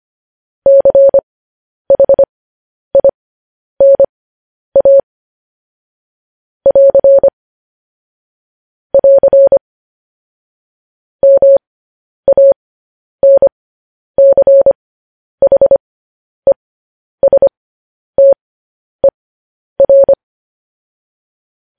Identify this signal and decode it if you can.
Morse code, heard as a keyed tone, about 25 wpm, spaced as for 10 wpm, 550 Hz. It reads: CHINA <AR> <AR> MANCHESTER
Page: news-headlines-in-morse-code-at-10-wpm